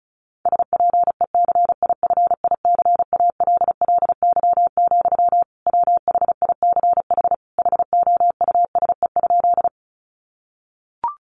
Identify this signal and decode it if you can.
Morse code, heard as a keyed tone, about 34 wpm, 700 Hz. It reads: SPECIFICALLY, WHICH HOUSE?
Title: Sentences from Top 1100 Words 35wpm